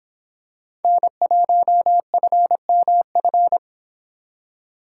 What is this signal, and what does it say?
Morse code, heard as a keyed tone, about 26 wpm, 700 Hz. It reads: N1FMF